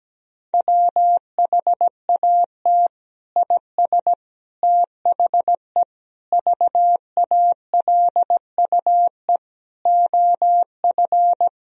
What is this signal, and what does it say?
Morse code, heard as a keyed tone, about 17 wpm, 700 Hz. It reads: WHAT IS THE VALUE OF
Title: Sentences from Top 1000 Words - Review Part 3 - 17wpm